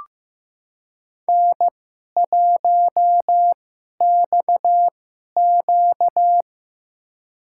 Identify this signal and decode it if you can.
Morse code, heard as a keyed tone, about 15 wpm, spaced as for 11 wpm, 700 Hz. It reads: N1XQ
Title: USA Call Signs - 2x Character Spacing - 15wpm